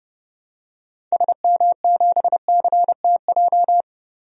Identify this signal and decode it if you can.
Morse code, heard as a keyed tone, about 30 wpm, 700 Hz. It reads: SM7CTJ
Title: DXCC-100 Call Signs - 6 Characters - Repeated 2x - 30wpm